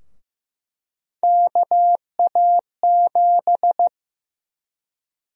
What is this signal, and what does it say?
Morse code, heard as a keyed tone, about 15 wpm, 700 Hz. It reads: KA7
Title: USA Call Sign Prefixes - 15wpm